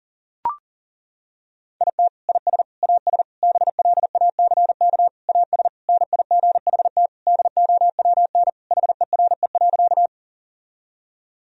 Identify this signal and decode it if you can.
Morse code, heard as a keyed tone, about 40 wpm, 700 Hz. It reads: IT IS AS BLACK AS NIGHT DOWN HERE.
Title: Sentences from Top 1000 Words - Review Part 2 - 40wpm